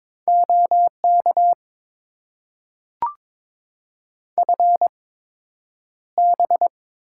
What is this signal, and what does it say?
Morse code, heard as a keyed tone, about 22 wpm, spaced as for 9 wpm, 700 Hz. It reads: OK FB
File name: Top 90 QSO Elements - 8x Character Spacing - 22wpm